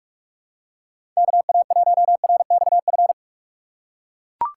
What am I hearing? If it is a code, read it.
Morse code, heard as a keyed tone, about 44 wpm, 700 Hz. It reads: KA1RXF
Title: USA Call Signs - 6 Characters - 6x Character Spacing - 45wpm